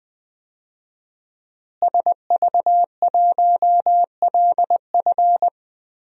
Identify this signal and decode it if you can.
Morse code, heard as a keyed tone, about 20 wpm, 700 Hz. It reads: SV1LF
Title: DXCC-100 Call Signs - 5 Characters - Repeated 3x - 20wpm